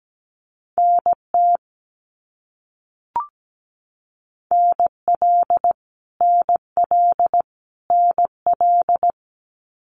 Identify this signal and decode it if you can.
Morse code, heard as a keyed tone, about 17 wpm, 700 Hz. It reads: NT NL NL NL
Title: Canadian Provinces and Territories - Repeated 3x - 17wpm